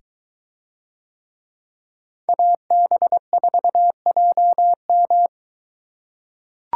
Morse code, heard as a keyed tone, about 21 wpm, 700 Hz. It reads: AB4JM